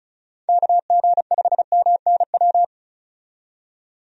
Morse code, heard as a keyed tone, about 34 wpm, 700 Hz. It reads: KG5MNW
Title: USA Call Signs - 6 Characters - 35wpm